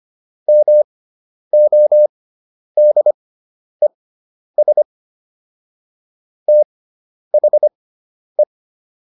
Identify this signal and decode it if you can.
Morse code, heard as a keyed tone, about 25 wpm, spaced as for 10 wpm, 600 Hz. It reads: MODES THE